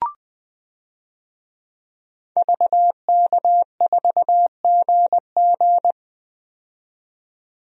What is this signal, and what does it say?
Morse code, heard as a keyed tone, about 20 wpm, 700 Hz. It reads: VK4GG
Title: DXCC-100 Call Signs - 5 Characters - 20wpm